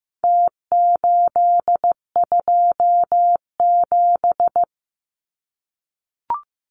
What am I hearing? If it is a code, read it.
Morse code, heard as a keyed tone, about 15 wpm, 700 Hz. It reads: T827
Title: Numbers - Contest Style - 4-Digit 15wpm